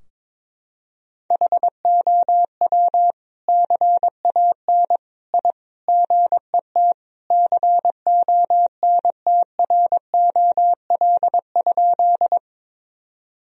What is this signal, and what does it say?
Morse code, heard as a keyed tone, about 22 wpm, 700 Hz. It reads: HOW CAN I GET CONTROL?
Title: Sentences from Top 1000 Words - Review Part 3 - 22wpm